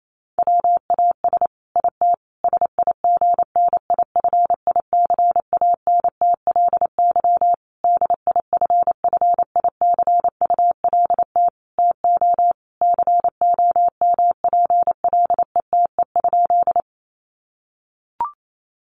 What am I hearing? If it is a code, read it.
Morse code, heard as a keyed tone, about 28 wpm, 700 Hz. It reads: WAS IT SIGNIFICANTLY DIFFICULT TO COMPLETE?
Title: Sentences from Top 1500 Words 28wpm